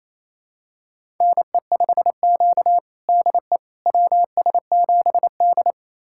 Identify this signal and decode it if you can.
Morse code, heard as a keyed tone, about 27 wpm, 690 Hz. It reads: NE5Q DE WS7D
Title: USA Callsign DE Callsign - 4 Characters - 28wpm